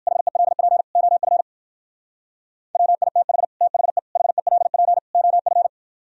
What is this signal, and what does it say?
Morse code, heard as a keyed tone, about 53 wpm, 700 Hz. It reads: HELP OF WITH THE HELP OF
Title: Speed-Racing - Sets of 4 Words 45wpm